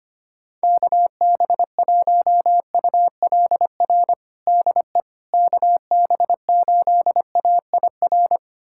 Morse code, heard as a keyed tone, about 25 wpm, 700 Hz. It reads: KB1ULR DE KB8AIR
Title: USA Callsign DE Callsign 25wpm